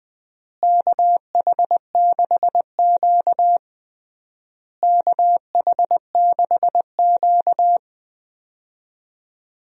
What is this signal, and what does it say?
Morse code, heard as a keyed tone, about 21 wpm, 700 Hz. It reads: KH6Q KH6Q
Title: DXCC-100 Call Signs - 4 Characters - Repeated 3x - 20wpm